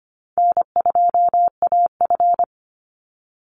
Morse code, heard as a keyed tone, about 25 wpm, 700 Hz. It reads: N2AF
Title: NA SOTA Chaser Callsigns 25wpm